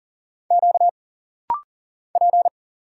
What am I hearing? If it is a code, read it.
Morse code, heard as a keyed tone, about 40 wpm, 700 Hz. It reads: Q P
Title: Single Letters - Rapid-Fire 40wpm